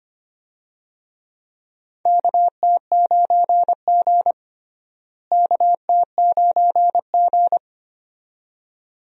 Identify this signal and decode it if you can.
Morse code, heard as a keyed tone, about 25 wpm, 700 Hz. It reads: KT9G KT9G
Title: USA Call Signs - 4 Characters - Repeated 2x - 25wpm